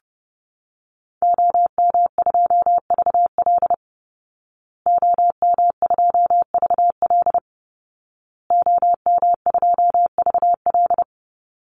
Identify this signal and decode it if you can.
Morse code, heard as a keyed tone, about 30 wpm, 700 Hz. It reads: OM2VL OM2VL OM2VL
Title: Call Signs Repeated 2x - 30wpm